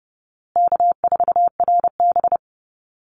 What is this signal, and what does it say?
Morse code, heard as a keyed tone, about 30 wpm, 700 Hz. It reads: K4RB
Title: Easy Call Signs Repeated 3x - 30wpm